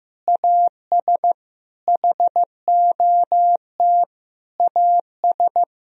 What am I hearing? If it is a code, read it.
Morse code, heard as a keyed tone, about 15 wpm, 700 Hz. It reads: AS HOT AS
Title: Sentences from Top 1000 Words - Review Part 4 - 15wpm